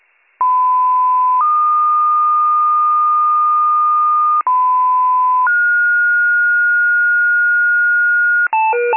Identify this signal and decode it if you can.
Two-tone page, heard as a one-second tone, then a three-second tone, about 1010 then 1220 Hz, about 1010 then 1430 Hz, with noise. It is heard from 0.4 to 8.5 s.